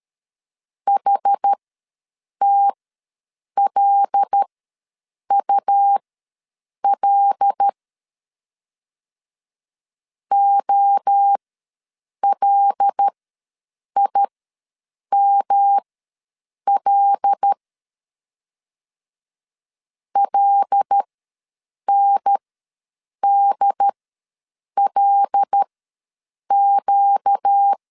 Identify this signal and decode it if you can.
Morse code, heard as a keyed tone, about 13 wpm, spaced as for 7 wpm, 800 Hz. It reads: HTLUL OLIML LNDLQ
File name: LEZIONI / 5 LBVC / 5 L ws 12wpm 2958